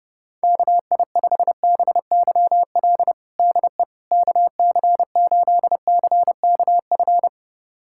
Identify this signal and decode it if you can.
Morse code, heard as a keyed tone, about 28 wpm, 700 Hz. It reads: KI5BYL DE KC8CKF
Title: USA Callsign DE Callsign - 6 Characters - 30wpm